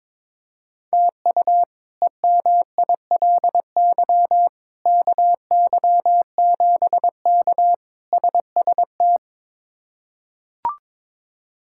Morse code, heard as a keyed tone, about 22 wpm, 700 Hz. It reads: TU EMILY KY7K SST
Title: K1USN SST QSO Practice - 22wpm